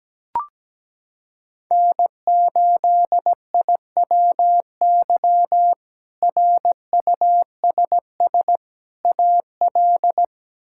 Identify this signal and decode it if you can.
Morse code, heard as a keyed tone, about 17 wpm, 700 Hz. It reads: N8IWY RUSS AL
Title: K1USN SST QSO Practice - 17wpm